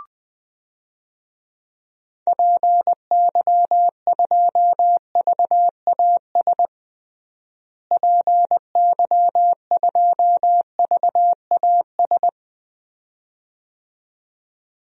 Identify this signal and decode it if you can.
Morse code, heard as a keyed tone, about 20 wpm, 700 Hz. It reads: PY2VAS PY2VAS
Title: DXCC-100 Call Signs - 6 Characters - Repeated 2x - 20wpm